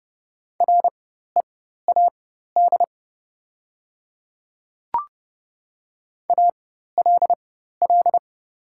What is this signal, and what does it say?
Morse code, heard as a keyed tone, about 30 wpm, spaced as for 14 wpm, 700 Hz. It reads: READ ALL
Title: 4x Character Spacing - Top 300 Words - 30wpm